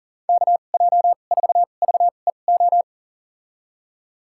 Morse code, heard as a keyed tone, about 39 wpm, spaced as for 29 wpm, 700 Hz. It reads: KJ4VEO
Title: USA Call Signs - 2x Character Spacing - 40wpm